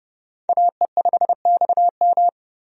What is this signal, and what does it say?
Morse code, heard as a keyed tone, about 30 wpm, 700 Hz. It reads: AE5XM
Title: USA Call Signs - 5 Characters - 30wpm